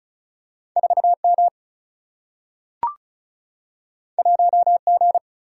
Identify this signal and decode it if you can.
Morse code, heard as a keyed tone, about 36 wpm, 700 Hz. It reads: 4M 1G
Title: Single-Letter Single-Number 35wpm